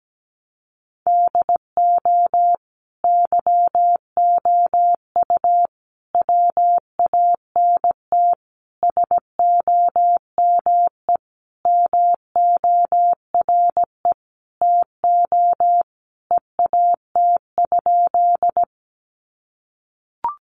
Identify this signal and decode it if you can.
Morse code, heard as a keyed tone, about 17 wpm, 700 Hz. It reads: DO YOU WANT SOME MORE TO EAT?
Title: Sentences from Top 300 Words 17wpm